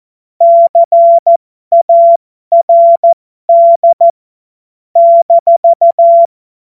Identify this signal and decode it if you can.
Morse code, heard as a keyed tone, about 14 wpm, spaced as for 12 wpm, 680 Hz. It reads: CARD -
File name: Morsekurs / Witze